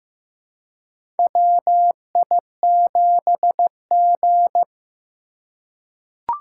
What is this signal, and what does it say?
Morse code, heard as a keyed tone, about 15 wpm, 700 Hz. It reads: WI7G
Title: USA Call Signs - 4 Characters - 4x Character Spacing - 15wpm